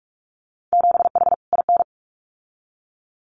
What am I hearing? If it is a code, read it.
Morse code, heard as a keyed tone, about 46 wpm, 700 Hz. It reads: 7H IN
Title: Winter Field Day Exchanges Encore 45wpm